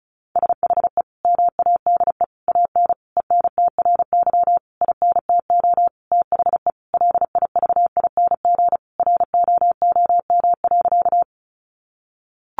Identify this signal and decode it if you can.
Morse code, heard as a keyed tone, about 34 wpm, 700 Hz. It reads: SHE MADE AN ENTRY INTO THE LIVING ROOM.
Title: Sentences from Top 1500 Words 35wpm